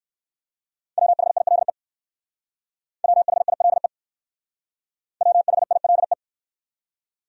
Morse code, heard as a keyed tone, about 58 wpm, 700 Hz. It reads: WHILE WHILE WHILE